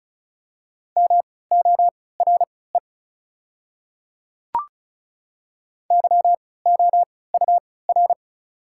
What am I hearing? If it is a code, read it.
Morse code, heard as a keyed tone, about 36 wpm, spaced as for 20 wpm, 690 Hz. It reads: MORE YOUR
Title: 3x Character Spacing - Top 100 Words - 35wpm